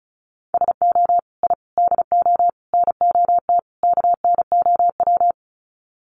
Morse code, heard as a keyed tone, about 34 wpm, 700 Hz. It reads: SO I DO NOT KNOW
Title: Sets of 5 Words Encore 35wpm